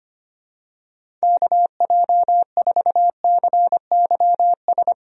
Morse code, heard as a keyed tone, about 25 wpm, 700 Hz. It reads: KJ4CYS
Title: USA Call Signs - 6 Characters - 25wpm